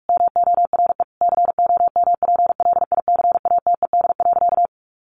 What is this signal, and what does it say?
Morse code, heard as a keyed tone, about 44 wpm, 700 Hz. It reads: MORE COMPLICATED.
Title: Sentences from Top 1200 Words 45wpm